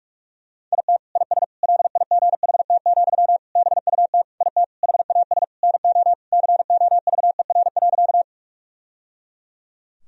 Morse code, heard as a keyed tone, about 42 wpm, 700 Hz. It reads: IT IS LIGHT, BUT IT HAS NO COVER.
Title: Sentences from Top 1000 Words - Review Part 1 - 45wpm